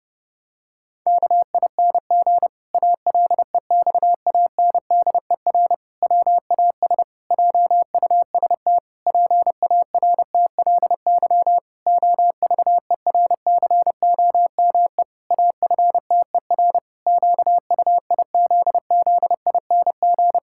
Morse code, heard as a keyed tone, about 32 wpm, 700 Hz. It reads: KING ALEXANDER WAS JUST PARTLY OVERCOME AFTER QUIZZING
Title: Pangram Challenge v2 - 30wpm